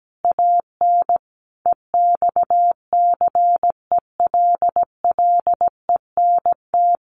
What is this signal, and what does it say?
Morse code, heard as a keyed tone, about 17 wpm, 700 Hz. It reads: AN EXCELLENT
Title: Sentences from Top 1000 Words 17wpm